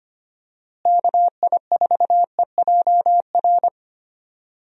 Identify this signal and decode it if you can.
Morse code, heard as a keyed tone, about 25 wpm, 700 Hz. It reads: KI4EJR
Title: USA Call Signs - 6 Characters - Repeated 3x - 25wpm